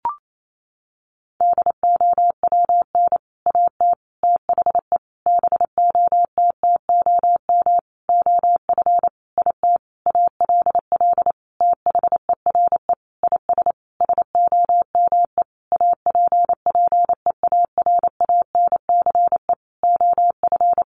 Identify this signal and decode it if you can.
Morse code, heard as a keyed tone, about 28 wpm, 700 Hz. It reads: DOWN AT THE BOTTOM OF IT ALL THERE IS SOME APPEARANCE OF
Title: Sentences from Top 1100 Words 28wpm